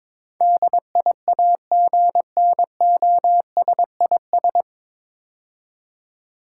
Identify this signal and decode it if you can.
Morse code, heard as a keyed tone, about 22 wpm, 700 Hz. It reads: DIAGNOSIS
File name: 9 Letter Words - First 3 Letters Given - 22wpm